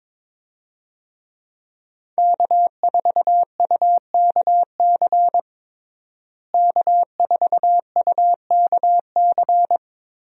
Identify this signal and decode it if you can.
Morse code, heard as a keyed tone, about 22 wpm, 700 Hz. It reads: K4UKC K4UKC
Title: USA Call Signs - 5 Characters - Repeated 3x - 22wpm